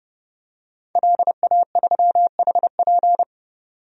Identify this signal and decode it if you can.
Morse code, heard as a keyed tone, about 30 wpm, 700 Hz. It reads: LA3HP
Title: DXCC-100 Call Signs - 5 Characters - 30wpm